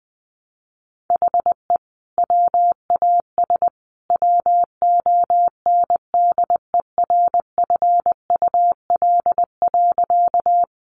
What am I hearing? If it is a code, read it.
Morse code, heard as a keyed tone, about 20 wpm, 700 Hz. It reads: HE WAS WONDERFUL.
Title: Sentences from Top 1100 Words 20wpm